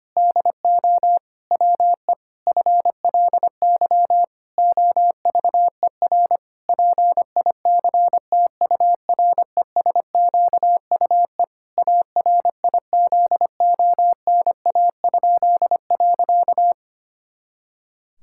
Morse code, heard as a keyed tone, about 25 wpm, 700 Hz. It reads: DO WE FLY OVER PICTURESQUE ARIZONA?.
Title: Pangram Challenge v2 - 25wpm